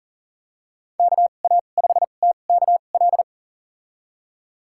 Morse code, heard as a keyed tone, about 40 wpm, spaced as for 29 wpm, 700 Hz. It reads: KA5TKL
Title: USA Call Signs - 6 Characters - 2x Character Spacing - 40wpm